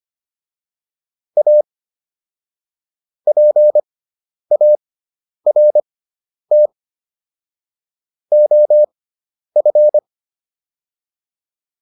Morse code, heard as a keyed tone, about 25 wpm, spaced as for 10 wpm, 600 Hz. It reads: A PART OF